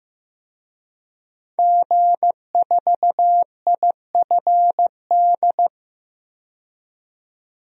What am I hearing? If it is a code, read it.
Morse code, heard as a keyed tone, about 15 wpm, 700 Hz. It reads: G4IFD
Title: DXCC-100 Call Signs - 5 Characters - 15wpm